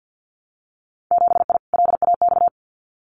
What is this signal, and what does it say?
Morse code, heard as a keyed tone, about 50 wpm, 700 Hz. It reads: 7I LAX
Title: Winter Field Day Exchanges Encore 50wpm